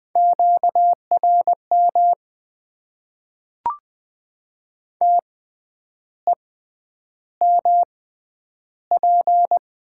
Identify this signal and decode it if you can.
Morse code, heard as a keyed tone, about 20 wpm, spaced as for 9 wpm, 700 Hz. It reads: QRM TEMP
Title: Top 90 QSO Elements - 6x Character Spacing - 20wpm